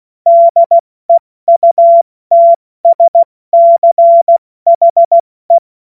Morse code, heard as a keyed tone, about 16 wpm, spaced as for 14 wpm, 680 Hz. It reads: DEUTSCHE
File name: Morsekurs / Witze / Witz 30. Dezember 2025 70 F80